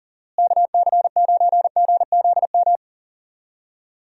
Morse code, heard as a keyed tone, about 40 wpm, 700 Hz. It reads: KC9GZM
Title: USA Call Signs - 6 Characters - Repeated 3x - 40wpm